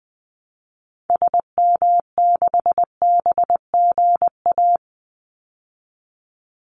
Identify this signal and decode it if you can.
Morse code, heard as a keyed tone, about 20 wpm, 700 Hz. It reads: SM6BGA
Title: Call Signs Repeated 2x - 20wpm